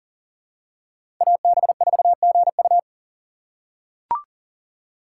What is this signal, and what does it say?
Morse code, heard as a keyed tone, about 37 wpm, 700 Hz. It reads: AB4GU
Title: USA Call Signs - 5 Characters - 4x Character Spacing - 40wpm